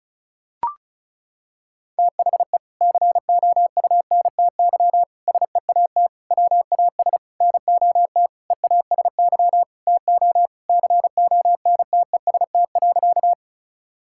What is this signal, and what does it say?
Morse code, heard as a keyed tone, about 36 wpm, 700 Hz. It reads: THE COUNTY SEAT WAS NOT EASY TO CONTEST.
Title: Sentences from Top 1600 Words 35wpm